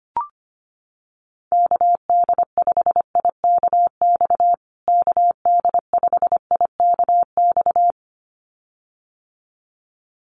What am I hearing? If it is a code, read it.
Morse code, heard as a keyed tone, about 25 wpm, 700 Hz. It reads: KD5IKX KD5IKX
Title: USA Call Signs Repeated 2x 25wpm